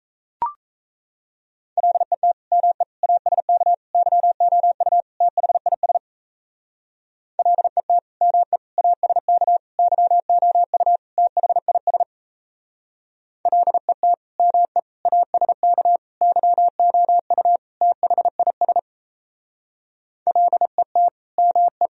Morse code, heard as a keyed tone, about 35 wpm, 700 Hz. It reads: LET ME ASK YOU THIS LET ME ASK YOU THIS LET ME ASK YOU THIS LET ME